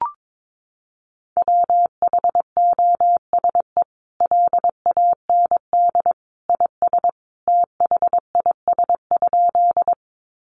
Morse code, heard as a keyed tone, about 22 wpm, 700 Hz. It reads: WHOSE LAND IS THIS?
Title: Sentences from Top 1200 Words 22wpm